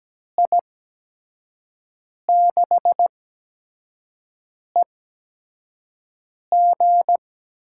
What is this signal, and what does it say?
Morse code, heard as a keyed tone, about 17 wpm, spaced as for 5 wpm, 700 Hz. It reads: I6EG
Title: USA Call Signs - 5 Characters - 8x Character Spacing - 17wpm